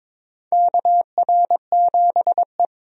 Morse code, heard as a keyed tone, about 22 wpm, 700 Hz. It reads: KR7E
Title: USA Call Signs - 4 Characters - 22wpm